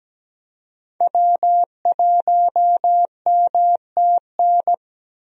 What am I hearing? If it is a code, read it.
Morse code, heard as a keyed tone, about 17 wpm, 700 Hz. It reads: W1MTN
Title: USA Call Signs - 5 Characters - Repeated 3x - 17wpm